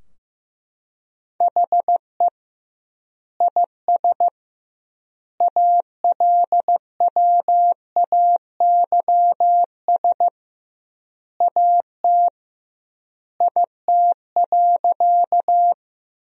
Morse code, heard as a keyed tone, about 15 wpm, 700 Hz. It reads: HE IS ALWAYS AT IT.
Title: 2x Word-Spacing - Sentences from Top 300 Words - 15wpm